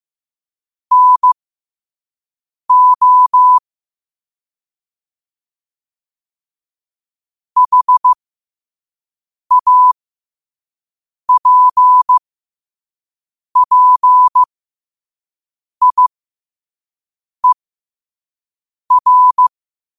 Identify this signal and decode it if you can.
Morse code, heard as a keyed tone, about 15 wpm, spaced as for 5 wpm, 1000 Hz. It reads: NO HAPPIER
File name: qotdfiles / 5wpm